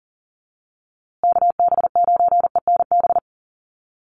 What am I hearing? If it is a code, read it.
Morse code, heard as a keyed tone, about 40 wpm, 700 Hz. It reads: KB9ENB
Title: USA Call Signs Repeated 3x 40wpm